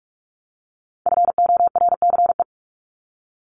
Morse code, heard as a keyed tone, about 44 wpm, 700 Hz. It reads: FORCE
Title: ED Suffix - Top 100 Words with Base Words 45wpm